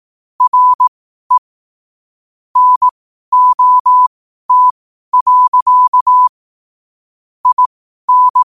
Morse code, heard as a keyed tone, about 18 wpm, spaced as for 12 wpm, 1000 Hz. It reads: RE NOT. IN
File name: qotdfiles / 12wpm